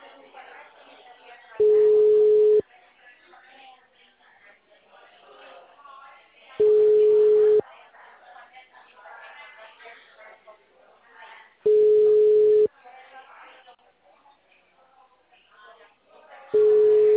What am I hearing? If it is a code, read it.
Ringback tone (Europe, 425 Hz), with speech.